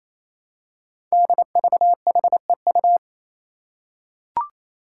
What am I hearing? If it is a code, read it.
Morse code, heard as a keyed tone, about 28 wpm, 700 Hz. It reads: DVHEU
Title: Five Challenging Letter-Number Combinations 28wpm